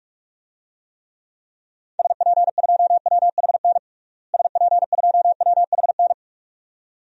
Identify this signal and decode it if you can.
Morse code, heard as a keyed tone, about 42 wpm, 700 Hz. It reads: SP2WHN SP2WHN
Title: DXCC-100 Call Signs - 6 Characters - Repeated 2x - 45wpm